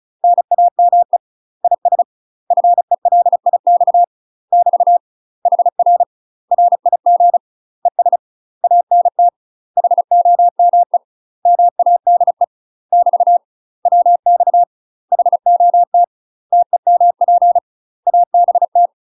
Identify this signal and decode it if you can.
Morse code, heard as a keyed tone, about 36 wpm, 700 Hz. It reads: NAME IS FELIX <BT> HR RIG ES ANT HOME MADE <BT> WX HOT TEMP ABT
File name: UK MASTER QSOs TRAINING / 35WPM / UK MASTER QSOs 0013